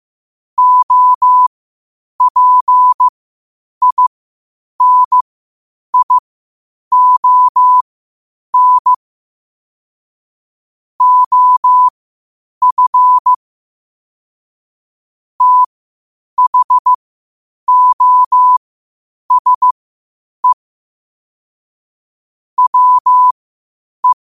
Morse code, heard as a keyed tone, about 15 wpm, spaced as for 8 wpm, 1000 Hz. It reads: OPINION OF THOSE WE
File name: qotdfiles / 8wpm